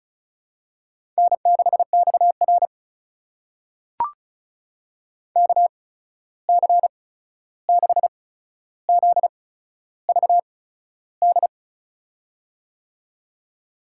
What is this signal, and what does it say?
Morse code, heard as a keyed tone, about 36 wpm, spaced as for 13 wpm, 700 Hz. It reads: N6XR KC6ZVD
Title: USA Call Signs - 8x Character Spacing - 35wpm